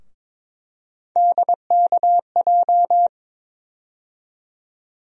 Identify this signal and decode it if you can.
Morse code, heard as a keyed tone, about 23 wpm, 700 Hz. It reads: DKJ
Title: USA Call Sign Postfixes - 3 Letters 22wpm